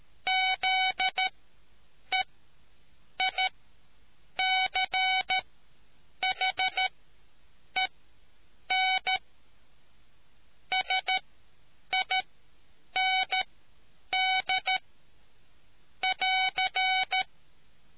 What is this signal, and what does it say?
Morse code, heard as a keyed tone, about 13 wpm, spaced as for 7 wpm, 750 Hz. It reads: ZEICHEN SIND <AR>